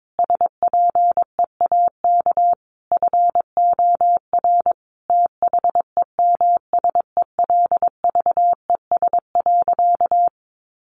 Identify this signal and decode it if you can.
Morse code, heard as a keyed tone, about 22 wpm, 700 Hz. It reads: SPEAK FOR THEMSELVES.
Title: Sentences from Top 1400 Words 22wpm